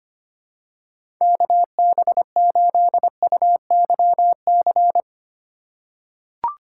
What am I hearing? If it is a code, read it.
Morse code, heard as a keyed tone, about 25 wpm, 700 Hz. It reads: KB8UYC
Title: USA Call Signs - 2x Character Spacing - 25wpm